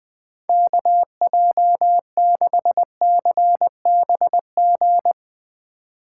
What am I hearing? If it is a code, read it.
Morse code, heard as a keyed tone, about 20 wpm, 700 Hz. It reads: KJ6CBG
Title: USA Call Signs - 2x Character Spacing - 20wpm